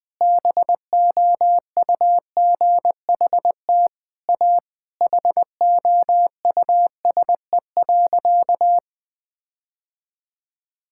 Morse code, heard as a keyed tone, about 20 wpm, 700 Hz. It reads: BOUGHT A HOUSE.
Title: Sentences with 4-Word Phrases - Spoken Beginning - 20wpm